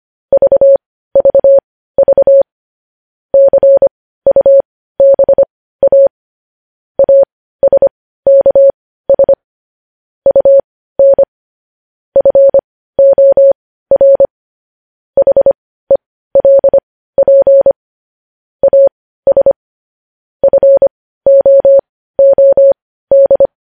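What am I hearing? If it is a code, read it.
Morse code, heard as a keyed tone, about 25 wpm, spaced as for 15 wpm, 550 Hz. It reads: VVV CUBA ASKS UN FOR HELP AS FOOD
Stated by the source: Morse code practice of news headlines at 15 wpm, opening with vvv.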